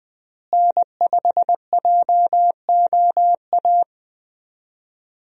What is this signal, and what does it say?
Morse code, heard as a keyed tone, about 20 wpm, 700 Hz. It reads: N5JOA
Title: USA Call Signs - 5 Characters - Repeated 3x - 20wpm